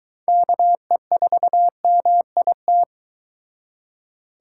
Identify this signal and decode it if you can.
Morse code, heard as a keyed tone, about 22 wpm, 700 Hz. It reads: KE4MIT